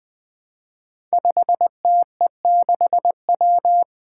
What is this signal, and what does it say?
Morse code, heard as a keyed tone, about 20 wpm, 700 Hz. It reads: 5TE6W
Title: Five Challenging Letter-Number Combinations 20wpm